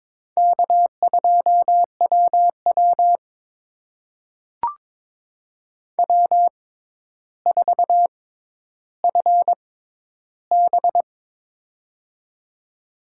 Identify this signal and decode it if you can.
Morse code, heard as a keyed tone, about 22 wpm, spaced as for 11 wpm, 700 Hz. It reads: K2WW W4FB
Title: USA Call Signs - 4 Characters - 6x Character Spacing - 22wpm